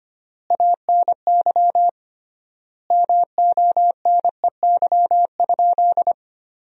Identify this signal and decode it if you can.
Morse code, heard as a keyed tone, about 25 wpm, 700 Hz. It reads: ANY MONEY?
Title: 3x Word-Spacing - Sentences from Top 400 Words - 25wpm